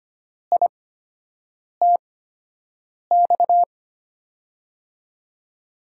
Morse code, heard as a keyed tone, about 25 wpm, spaced as for 7 wpm, 700 Hz. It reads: ITX
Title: USA Call Signs - 6 Characters - 8x Character Spacing - 25wpm